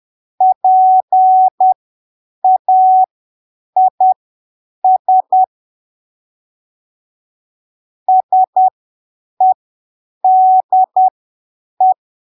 Morse code, heard as a keyed tone, about 10 wpm, spaced as for 7 wpm, 750 Hz. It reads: PAIS SEDE
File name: arquivos / texto / texto 08 10ppm